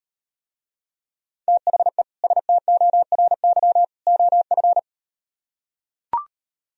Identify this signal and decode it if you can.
Morse code, heard as a keyed tone, about 38 wpm, 700 Hz. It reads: THE STORY OF